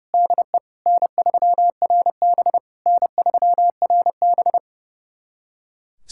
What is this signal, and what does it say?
Morse code, heard as a keyed tone, about 30 wpm, 700 Hz. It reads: DE N3RB N3RB
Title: SOTA QSO Practice 30wpm